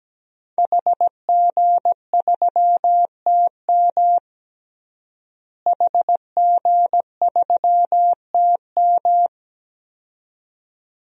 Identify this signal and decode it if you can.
Morse code, heard as a keyed tone, about 17 wpm, 700 Hz. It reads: HG3TM HG3TM
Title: DXCC-100 Call Signs - 5 Characters - Repeated 2x - 17wpm